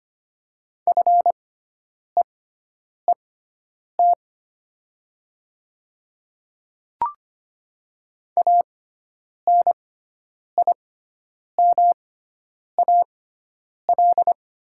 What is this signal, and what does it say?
Morse code, heard as a keyed tone, about 25 wpm, spaced as for 9 wpm, 700 Hz. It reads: FEET ANIMAL
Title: 6x Character Spacing - Top 500 Words - 25wpm